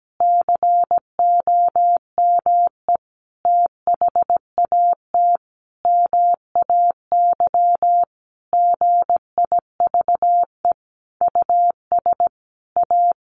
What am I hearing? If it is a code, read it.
Morse code, heard as a keyed tone, about 17 wpm, 700 Hz. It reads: COME THAT MAY GIVE US A
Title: Sentences from Top 1100 Words 17wpm